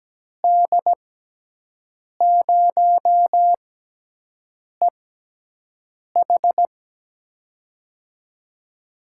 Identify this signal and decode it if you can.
Morse code, heard as a keyed tone, about 17 wpm, spaced as for 6 wpm, 700 Hz. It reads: D0EH
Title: USA Call Signs - 5 Characters - 6x Character Spacing - 17wpm